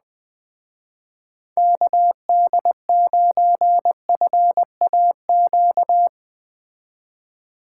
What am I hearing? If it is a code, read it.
Morse code, heard as a keyed tone, about 21 wpm, 700 Hz. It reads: KD9FAQ
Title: Speed Racing - USA Call Signs 17wpm